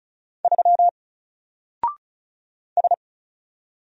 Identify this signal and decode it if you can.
Morse code, heard as a keyed tone, about 35 wpm, 700 Hz. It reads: 3 S
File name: Single-Letter-Number-35wpm